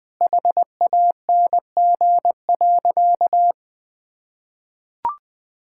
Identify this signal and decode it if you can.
Morse code, heard as a keyed tone, about 19 wpm, 700 Hz. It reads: HANG.
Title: Sentences from Top 1600 Words 20wpm